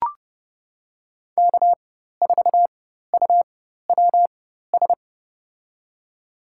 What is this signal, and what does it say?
Morse code, heard as a keyed tone, about 32 wpm, spaced as for 14 wpm, 700 Hz. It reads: K4UWS
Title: USA Call Signs - 5 Characters - 4x Character Spacing - 30wpm